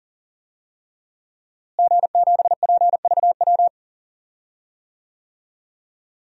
Morse code, heard as a keyed tone, about 40 wpm, 700 Hz. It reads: G7PVW
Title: DXCC-100 Call Signs - 5 Characters - Repeated 3x - 40wpm